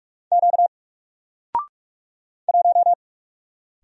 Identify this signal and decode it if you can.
Morse code, heard as a keyed tone, about 46 wpm, 700 Hz. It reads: Q 1
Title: Single Letter-Number Alt 45wpm